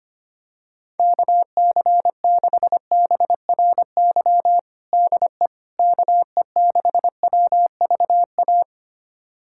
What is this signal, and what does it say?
Morse code, heard as a keyed tone, about 25 wpm, 700 Hz. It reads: KC6BRY DE KE6WVA
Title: USA Callsign DE Callsign - 6 Characters - 25wpm